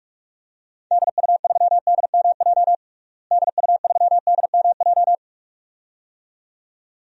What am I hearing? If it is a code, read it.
Morse code, heard as a keyed tone, about 41 wpm, 700 Hz. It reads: DU3DMJ DU3DMJ
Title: DXCC-100 Call Signs - 6 Characters - Repeated 2x - 45wpm